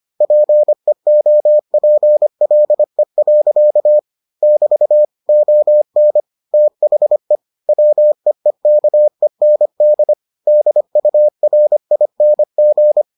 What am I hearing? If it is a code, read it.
Morse code, heard as a keyed tone, about 25 wpm, 600 Hz. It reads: PEOPLE. <BT> ON THE WEEKEND DURING